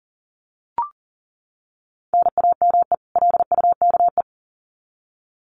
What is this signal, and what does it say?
Morse code, heard as a keyed tone, about 40 wpm, 700 Hz. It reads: NAME LUKE
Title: Common QSO Phrases 40wpm